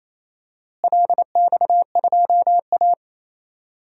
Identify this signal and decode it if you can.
Morse code, heard as a keyed tone, about 28 wpm, 700 Hz. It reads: LX2A
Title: DXCC-100 Call Signs - 4 Characters - Repeated 2x - 28wpm